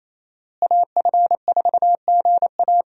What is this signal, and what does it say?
Morse code, heard as a keyed tone, about 27 wpm, 700 Hz. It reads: AF4GA
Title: USA Call Signs - 5 Characters - Repeated 2x - 28wpm